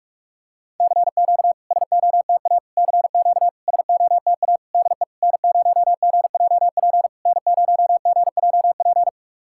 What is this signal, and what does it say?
Morse code, heard as a keyed tone, about 44 wpm, 690 Hz. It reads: CQ SOTA CQ SOTA DE N0GJP N0GJP
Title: SOTA QSO Practice 45wpm